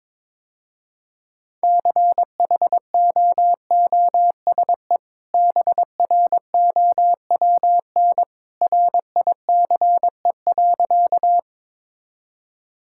Morse code, heard as a keyed tone, about 22 wpm, 700 Hz. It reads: CHOOSE BROWN RICE.
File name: practice / 22wpm / Easy Sentences with Hints - 22wpm